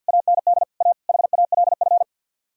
Morse code, heard as a keyed tone, about 64 wpm, 700 Hz. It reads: AND A HALF